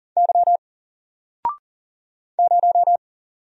Morse code, heard as a keyed tone, about 40 wpm, 700 Hz. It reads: Y 0
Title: Single Letter-Number Alt 40wpm